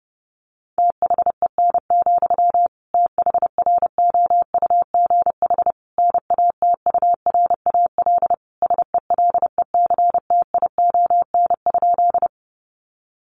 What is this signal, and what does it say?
Morse code, heard as a keyed tone, about 30 wpm, 700 Hz. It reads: THEN, THROUGH NATURAL SELECTION?
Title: Sentences from Top 1400 Words 30wpm